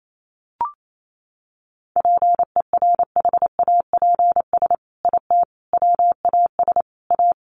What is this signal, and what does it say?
Morse code, heard as a keyed tone, about 28 wpm, 700 Hz. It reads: PERHAPS IT WAS A
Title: Sentences from Top 1500 Words 28wpm